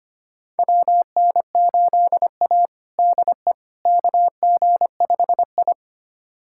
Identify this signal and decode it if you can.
Morse code, heard as a keyed tone, about 25 wpm, 700 Hz. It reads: WN8A DE KG5I
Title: USA Callsign DE Callsign - 4 Characters - 25wpm